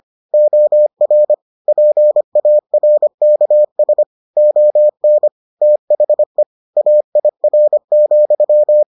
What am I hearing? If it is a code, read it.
Morse code, heard as a keyed tone, about 25 wpm, 600 Hz. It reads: OR PARKS ON THE AIR,